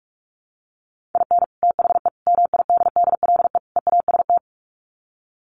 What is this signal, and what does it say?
Morse code, heard as a keyed tone, about 44 wpm, 700 Hz. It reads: IN THE MIDDLE EAST
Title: Sets of 4 Words 45wpm